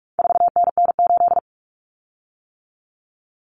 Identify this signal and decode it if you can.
Morse code, heard as a keyed tone, about 44 wpm, 700 Hz. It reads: 4NN8
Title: Common Cut Numbers - 4-Digit 45wpm